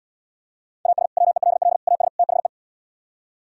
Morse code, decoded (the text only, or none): AS WELL AS THE